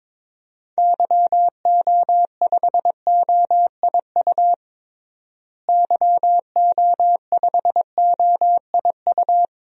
Morse code, heard as a keyed tone, about 22 wpm, 700 Hz. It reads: YO5OIU YO5OIU
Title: DXCC-100 Call Signs - 6 Characters - Repeated 3x - 22wpm